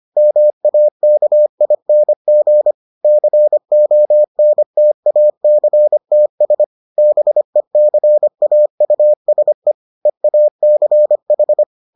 Morse code, heard as a keyed tone, about 25 wpm, 600 Hz. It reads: MAKING CONTACTS BECAUSE EACH